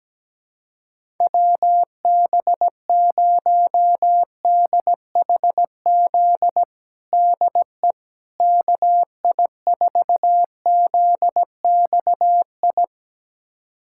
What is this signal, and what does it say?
Morse code, heard as a keyed tone, about 17 wpm, 700 Hz. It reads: WB0DHZ DE KI4ZXI
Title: USA Callsign DE Callsign - 6 Characters - 17wpm